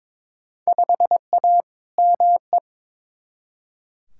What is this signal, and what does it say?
Morse code, heard as a keyed tone, about 22 wpm, 700 Hz. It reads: 5A ME
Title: Field Day Exchanges Encore - 22wpm